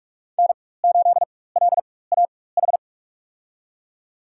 Morse code, heard as a keyed tone, about 41 wpm, spaced as for 20 wpm, 700 Hz. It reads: N8LAH
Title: USA Call Signs - 5 Characters - 4x Character Spacing - 45wpm